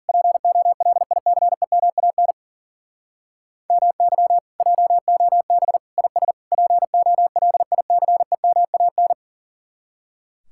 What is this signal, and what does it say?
Morse code, heard as a keyed tone, about 53 wpm, 700 Hz. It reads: POLICEMAN MY JOB IS POLICEMAN